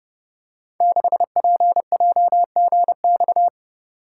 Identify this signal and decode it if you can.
Morse code, heard as a keyed tone, about 30 wpm, 700 Hz. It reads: 6PJGX
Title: Code Groups - 5 Letter:Numbers - 30wpm